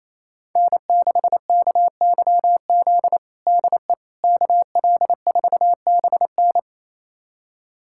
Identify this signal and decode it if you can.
Morse code, heard as a keyed tone, about 27 wpm, 700 Hz. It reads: N6KYZ DE KL4BN
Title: USA Callsign DE Callsign - 5 Characters - 28wpm